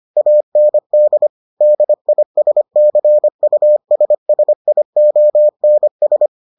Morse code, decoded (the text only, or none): AND DISCUSSIONS